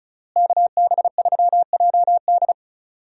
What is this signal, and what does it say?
Morse code, heard as a keyed tone, about 36 wpm, 700 Hz. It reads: KB3JD
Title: USA Call Signs - 5 Characters - Repeated 3x - 35wpm